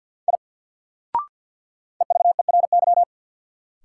Morse code, heard as a keyed tone, about 47 wpm, 700 Hz. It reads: I EVERY
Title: Sentences from Top 200 Words - New Word Drill - 50wpm